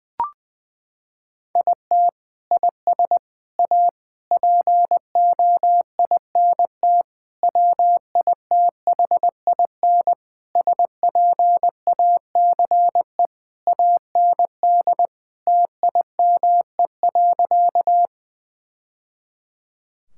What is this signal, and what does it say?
Morse code, heard as a keyed tone, about 20 wpm, 700 Hz. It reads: IT IS A POINT WITHIN SPACE AND TIME.
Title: Sentences from Top 1000 Words - Review Part 5 - 20wpm